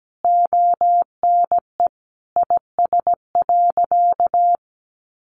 Morse code, heard as a keyed tone, about 17 wpm, 700 Hz. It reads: ONE IS.
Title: Sentences from Top 1100 Words 17wpm